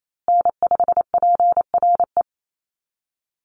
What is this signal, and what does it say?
Morse code, heard as a keyed tone, about 28 wpm, 700 Hz. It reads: N5PRE
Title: USA Call Signs Repeated 2x 28wpm